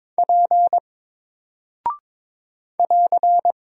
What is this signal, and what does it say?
Morse code, heard as a keyed tone, about 22 wpm, 700 Hz. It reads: P <AR>
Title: Single Character (Letter:Number:Punctuation:Prosign) 22wpm